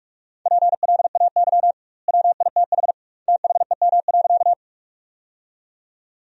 Morse code, heard as a keyed tone, about 42 wpm, 700 Hz. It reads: PLAY WITH THEM.
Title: 2x Word-Spacing - Sentences from Top 200 Words - 45wpm